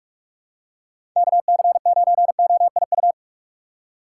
Morse code, heard as a keyed tone, about 46 wpm, 700 Hz. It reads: KC9OIU
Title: USA Call Signs - 6 Characters - 3x Character Spacing - 45wpm